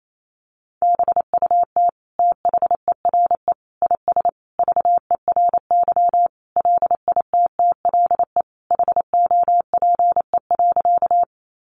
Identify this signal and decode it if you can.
Morse code, heard as a keyed tone, about 28 wpm, 700 Hz. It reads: BUT THERE IS VERY LITTLE HOPE.
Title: Sentences from Top 1000 Words 28wpm